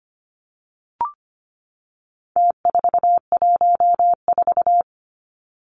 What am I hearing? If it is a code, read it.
Morse code, heard as a keyed tone, about 25 wpm, 700 Hz. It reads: T414
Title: Numbers - Contest Style - 4-Digit 25wpm